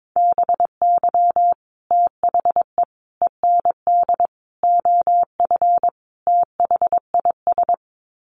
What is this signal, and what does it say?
Morse code, heard as a keyed tone, about 22 wpm, 700 Hz. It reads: BY THE END OF THIS
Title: Sets of 5 Words Encore 22wpm